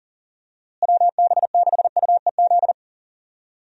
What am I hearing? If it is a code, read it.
Morse code, heard as a keyed tone, about 40 wpm, 700 Hz. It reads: WB6UEZ
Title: USA Call Signs - 6 Characters - Repeated 2x - 40wpm